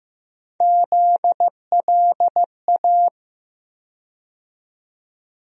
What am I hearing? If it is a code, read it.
Morse code, heard as a keyed tone, about 15 wpm, 700 Hz. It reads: ZLA
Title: USA Call Sign Postfixes - 3 Letters 15wpm